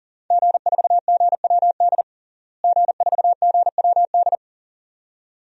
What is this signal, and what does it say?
Morse code, heard as a keyed tone, about 40 wpm, 700 Hz. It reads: G4GWD G4GWD
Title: DXCC-100 Call Signs - 5 Characters - Repeated 2x - 40wpm